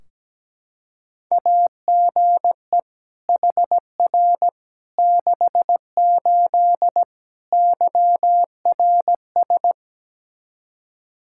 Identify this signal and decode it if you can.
Morse code, heard as a keyed tone, about 17 wpm, 700 Hz. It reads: AGE HR 68 YRS
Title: K1USN SST QSO Practice 17wpm